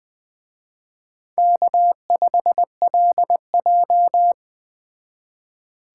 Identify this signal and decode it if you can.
Morse code, heard as a keyed tone, about 20 wpm, 700 Hz. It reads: K5LJ
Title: Speed Racing - USA 1x2 Call Signs 17wpm